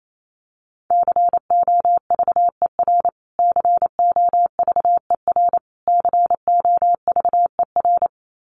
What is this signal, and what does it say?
Morse code, heard as a keyed tone, about 28 wpm, 700 Hz. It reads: COVER COVER COVER
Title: Top 500 Words Repeated 28wpm